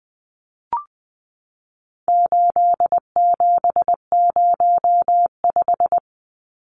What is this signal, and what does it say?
Morse code, heard as a keyed tone, about 20 wpm, 700 Hz. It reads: 8705
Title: Numbers - 4-Digit 20wpm